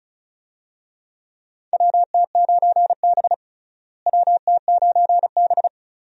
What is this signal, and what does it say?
Morse code, heard as a keyed tone, about 36 wpm, 700 Hz. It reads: WT9B WT9B
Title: USA Call Signs - 4 Characters - Repeated 3x - 35wpm